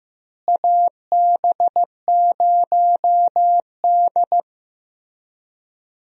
Morse code, heard as a keyed tone, about 15 wpm, 700 Hz. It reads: AB0D
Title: USA Call Signs - 4 Characters - Repeated 2x - 15wpm